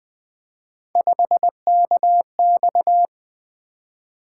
Morse code, heard as a keyed tone, about 20 wpm, 700 Hz. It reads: 5KX
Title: Three Letter-Number Combinations - 20wpm